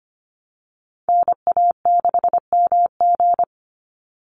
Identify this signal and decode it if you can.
Morse code, heard as a keyed tone, about 25 wpm, 700 Hz. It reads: NA6MG
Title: NA SOTA Chaser Callsigns 25wpm